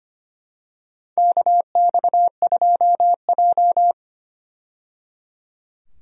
Morse code, heard as a keyed tone, about 25 wpm, 700 Hz. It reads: KX2J
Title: USA Call Signs - 4 Characters - 25wpm